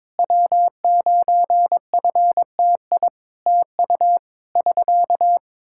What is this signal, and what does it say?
Morse code, heard as a keyed tone, about 22 wpm, 700 Hz. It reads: W9FTI TU <SK>
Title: Practice QSO - Exchange 4 - Ending - 22wpm